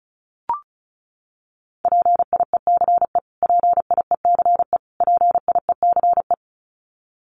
Morse code, heard as a keyed tone, about 34 wpm, 700 Hz. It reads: PIECE PIECE PIECE
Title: Top 500 Words Repeated 35wpm